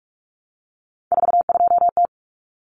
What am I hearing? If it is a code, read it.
Morse code, heard as a keyed tone, about 44 wpm, 700 Hz. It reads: 42T
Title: Common Cut Numbers - 3-Digit 45wpm